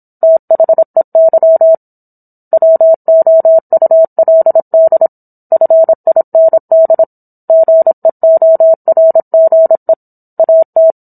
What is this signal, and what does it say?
Morse code, heard as a keyed tone, about 27 wpm, 650 Hz. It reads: THEY WOULD FIND GEORGE AT